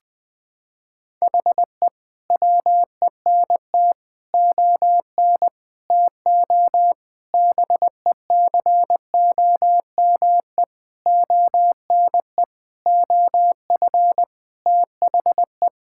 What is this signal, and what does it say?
Morse code, heard as a keyed tone, about 20 wpm, 700 Hz. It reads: HE WENT ON TO BECOME ONE OF THE
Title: Sentences with 7-Word Phrases - 20wpm